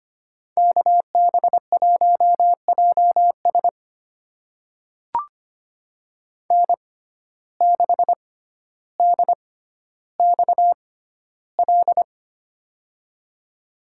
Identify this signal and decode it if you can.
Morse code, heard as a keyed tone, about 25 wpm, spaced as for 13 wpm, 700 Hz. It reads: KB1JS N6DXL
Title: USA Call Signs - 5 Characters - 6x Character Spacing - 25wpm